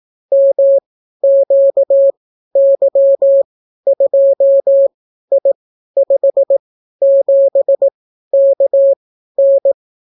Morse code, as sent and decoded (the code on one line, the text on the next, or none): -- --.- -.-- ..--- .. ..... --... -.- -.
MQY2I57KN